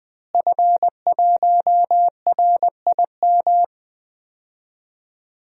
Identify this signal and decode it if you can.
Morse code, heard as a keyed tone, about 21 wpm, 700 Hz. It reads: F1RIM
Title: DXCC-100 Call Signs - 5 Characters - Repeated 2x - 20wpm